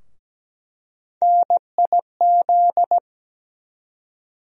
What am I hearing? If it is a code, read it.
Morse code, heard as a keyed tone, about 17 wpm, 700 Hz. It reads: NIZ
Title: USA Call Sign Postfixes - 3 Letters 17wpm